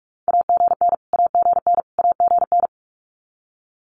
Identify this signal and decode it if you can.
Morse code, heard as a keyed tone, about 46 wpm, 700 Hz. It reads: AGN AGN AGN
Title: Abbreviations Repeated 3x - 45wpm